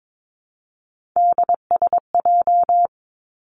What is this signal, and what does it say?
Morse code, heard as a keyed tone, about 22 wpm, 700 Hz. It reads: DSJ
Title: Three Challenging Letter-Number Combinations 22wpm